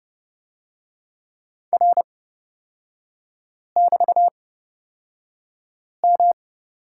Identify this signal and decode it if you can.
Morse code, heard as a keyed tone, about 30 wpm, spaced as for 5 wpm, 700 Hz. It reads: R<BT>M
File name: Lesson 36e - BT - Three Characters 30wpm